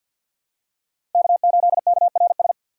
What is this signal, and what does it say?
Morse code, heard as a keyed tone, about 51 wpm, 700 Hz. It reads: K8KRS
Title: USA Call Signs - 5 Characters - 4x Character Spacing - 50wpm